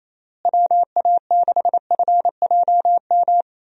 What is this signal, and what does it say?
Morse code, heard as a keyed tone, about 28 wpm, 700 Hz. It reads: WA6FJM
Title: USA Call Signs - 6 Characters - Repeated 2x - 28wpm